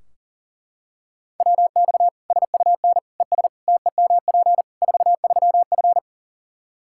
Morse code, heard as a keyed tone, about 37 wpm, 700 Hz. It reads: WX SUN ES TEMP 43F
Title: Practice QSO - Exchange 2 - Info - 40wpm